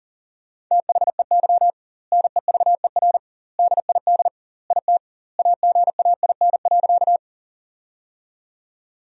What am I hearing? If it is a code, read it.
Morse code, heard as a keyed tone, about 40 wpm, 700 Hz. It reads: THEY NEVER DID IT AGAIN.
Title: 2x Word-Spacing - Sentences from Top 300 Words - 40wpm